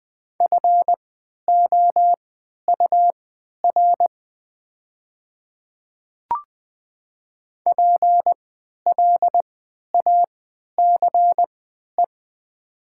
Morse code, heard as a keyed tone, about 20 wpm, spaced as for 11 wpm, 700 Hz. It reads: FOUR PLACE
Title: 3x Character Spacing - Top 300 Words - 20wpm